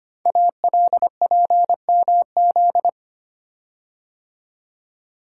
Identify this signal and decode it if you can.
Morse code, heard as a keyed tone, about 25 wpm, 700 Hz. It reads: ALPMZ
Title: Code Groups - 5 Letter:Numbers - 25wpm